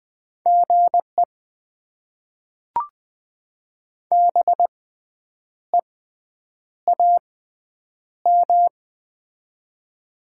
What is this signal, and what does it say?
Morse code, heard as a keyed tone, about 20 wpm, spaced as for 8 wpm, 700 Hz. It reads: GE BEAM
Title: Top 90 QSO Elements - 6x Character Spacing - 20wpm